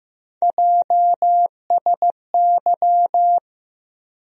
Morse code, heard as a keyed tone, about 15 wpm, 700 Hz. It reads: JSY
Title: USA Call Sign Postfixes - 3 Letters 15wpm